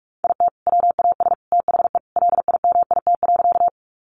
Morse code, heard as a keyed tone, about 44 wpm, 700 Hz. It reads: IT WAS THE LIMIT.
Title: Sentences from Top 1500 Words 45wpm